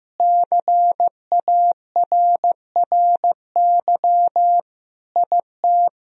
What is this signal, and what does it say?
Morse code, heard as a keyed tone, about 15 wpm, 700 Hz. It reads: CARRY IT
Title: Sentences from Top 1000 Words - Review Part 1 - 15wpm